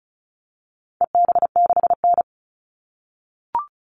Morse code, heard as a keyed tone, about 34 wpm, 700 Hz. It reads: EB6N
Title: All Cut Numbers - 4-Digit 35wpm